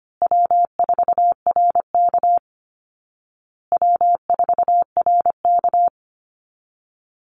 Morse code, heard as a keyed tone, about 25 wpm, 700 Hz. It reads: W4RK W4RK
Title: Call Signs Repeated 2x - 25wpm